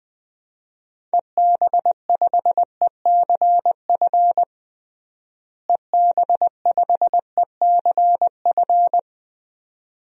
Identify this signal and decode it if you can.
Morse code, heard as a keyed tone, about 20 wpm, 700 Hz. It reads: EB5ECF EB5ECF
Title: DXCC-100 Call Signs - 6 Characters - Repeated 2x - 20wpm